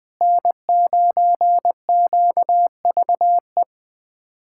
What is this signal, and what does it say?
Morse code, heard as a keyed tone, about 20 wpm, 700 Hz. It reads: N9QVE
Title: USA Callsign DE Callsign 20wpm